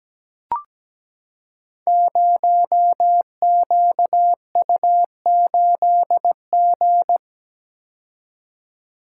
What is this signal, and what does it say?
Morse code, heard as a keyed tone, about 17 wpm, 700 Hz. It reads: 0QU8G
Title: Code Groups - 5 Letter:Numbers - 17wpm